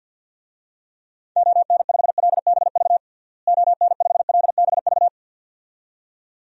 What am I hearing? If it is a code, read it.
Morse code, heard as a keyed tone, about 50 wpm, 700 Hz. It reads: ON5LBV ON5LBV
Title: DXCC-100 Call Signs - 6 Characters - Repeated 2x - 50wpm